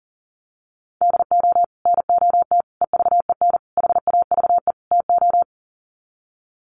Morse code, heard as a keyed tone, about 40 wpm, 700 Hz. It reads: DO NOT EVEN HAVE TO
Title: Sets of 5 Words Encore2 40wpm